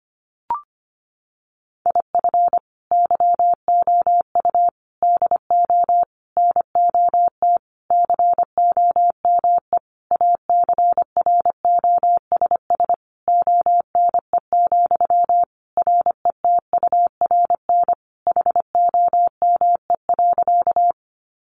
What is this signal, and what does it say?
Morse code, heard as a keyed tone, about 25 wpm, 700 Hz. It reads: IF YOU DO NOT COME ACROSS ONE, RETURN HOME.
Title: Sentences from Top 800 Words 25wpm